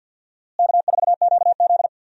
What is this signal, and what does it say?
Morse code, heard as a keyed tone, about 49 wpm, 700 Hz. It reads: K4QZ
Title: USA Call Signs - 4 Characters - Repeated 3x - 50wpm